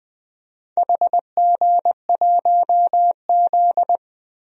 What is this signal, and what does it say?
Morse code, heard as a keyed tone, about 20 wpm, 700 Hz. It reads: HG1Z
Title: DXCC-100 Call Signs - 4 Characters - 20wpm